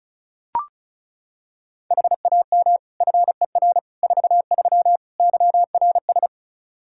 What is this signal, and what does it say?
Morse code, heard as a keyed tone, about 34 wpm, 700 Hz. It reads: HAM FER 43 YRS
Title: Practice QSO - Exchange 3 - More Info - 35wpm